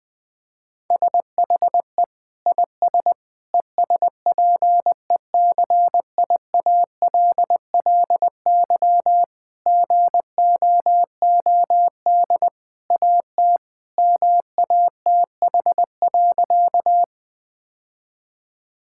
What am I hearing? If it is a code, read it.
Morse code, heard as a keyed tone, about 20 wpm, 700 Hz. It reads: SHE IS ESPECIALLY GOOD AT MATH.
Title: Sentences from Top 1000 Words - Review Part 6 - 20wpm